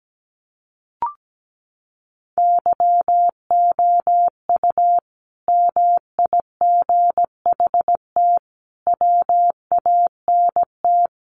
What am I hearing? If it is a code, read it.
Morse code, heard as a keyed tone, about 17 wpm, 700 Hz. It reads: YOU MIGHT WANT
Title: Sentences from Top 300 Words 17wpm